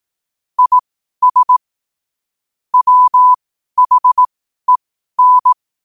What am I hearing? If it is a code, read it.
Morse code, heard as a keyed tone, about 18 wpm, spaced as for 12 wpm, 1000 Hz. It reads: IS WHEN
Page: Quotes for Fri, 15 Aug 2025 in Morse Code at 12 words per minute.